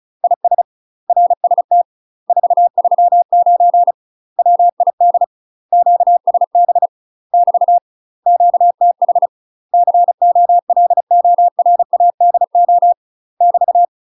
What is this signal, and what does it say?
Morse code, heard as a keyed tone, about 34 wpm, 700 Hz. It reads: IS RST 439 WID QSB <BT> QTH COLORADO <BT>
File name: UK MASTER QSOs TRAINING/35WPM